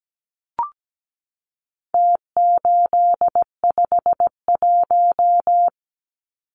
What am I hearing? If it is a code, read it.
Morse code, heard as a keyed tone, about 17 wpm, 700 Hz. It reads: T851
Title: Numbers - Contest Style - 4-Digit 17wpm